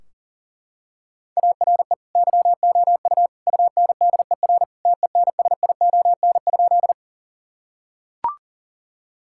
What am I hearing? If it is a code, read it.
Morse code, heard as a keyed tone, about 40 wpm, 700 Hz. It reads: ARE YOU UNDER TENSION?
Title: Sentences from Top 1600 Words 40wpm